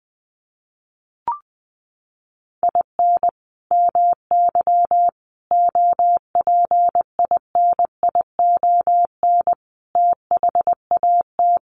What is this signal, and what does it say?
Morse code, heard as a keyed tone, about 19 wpm, 700 Hz. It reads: IN MY OPINION THAT
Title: Sentences from Top 1200 Words 20wpm